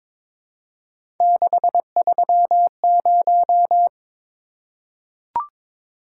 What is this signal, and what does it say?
Morse code, heard as a keyed tone, about 22 wpm, 700 Hz. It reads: 630
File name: practice / 22wpm / Three Letter-Number Combinations - 22wpm